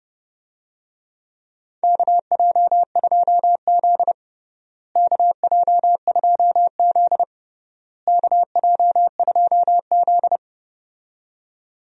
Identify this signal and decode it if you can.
Morse code, heard as a keyed tone, about 30 wpm, 700 Hz. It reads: KJ2Z KJ2Z KJ2Z
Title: USA Call Signs - 4 Characters - Repeated 3x - 30wpm